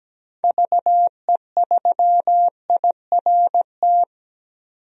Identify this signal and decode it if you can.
Morse code, heard as a keyed tone, about 17 wpm, 700 Hz. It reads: VE3IRT
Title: DXCC-100 Call Signs - 6 Characters - 17wpm